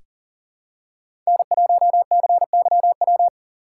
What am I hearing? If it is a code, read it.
Morse code, heard as a keyed tone, about 40 wpm, 700 Hz. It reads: N1CYW
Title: USA Call Signs - 5 Characters - 2x Character Spacing - 40wpm